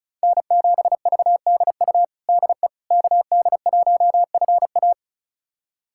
Morse code, heard as a keyed tone, about 34 wpm, 700 Hz. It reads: N7VDU DE KD1FA
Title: USA Callsign DE Callsign - 5 Characters - 35wpm